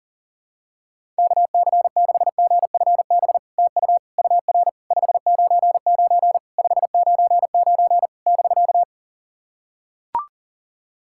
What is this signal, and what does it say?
Morse code, heard as a keyed tone, about 40 wpm, 700 Hz. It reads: KC6GFB TU UR 599 599 <BK>
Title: SOTA QSO Practice 40wpm